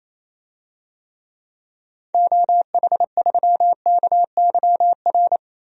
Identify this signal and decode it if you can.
Morse code, heard as a keyed tone, about 28 wpm, 700 Hz. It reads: OH3KYR
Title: DXCC-100 Call Signs - 6 Characters - 28wpm